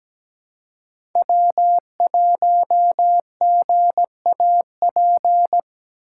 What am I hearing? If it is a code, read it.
Morse code, heard as a keyed tone, about 17 wpm, 700 Hz. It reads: W1GAP
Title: USA Call Signs - 5 Characters - 4x Character Spacing - 17wpm